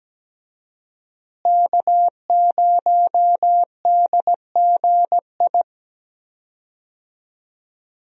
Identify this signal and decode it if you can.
Morse code, heard as a keyed tone, about 17 wpm, 700 Hz. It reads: K0DGI
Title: USA Call Signs - 5 Characters - Repeated 2x - 17wpm